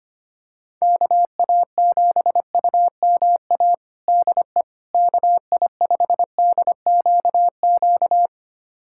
Morse code, heard as a keyed tone, about 25 wpm, 700 Hz. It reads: KA7UMA DE KI5DQQ
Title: USA Callsign DE Callsign - 6 Characters - 25wpm